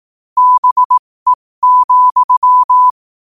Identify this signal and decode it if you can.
Morse code, heard as a keyed tone, about 18 wpm, spaced as for 15 wpm, 1000 Hz. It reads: BE,
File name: qotdfiles / 15wpm